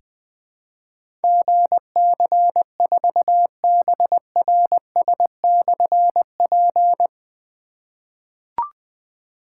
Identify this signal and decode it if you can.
Morse code, heard as a keyed tone, about 20 wpm, 700 Hz. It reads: GC4BRS/P
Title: Difficult Call Signs 20wpm